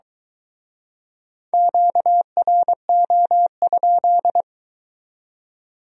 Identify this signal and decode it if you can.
Morse code, heard as a keyed tone, about 23 wpm, 700 Hz. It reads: QRO?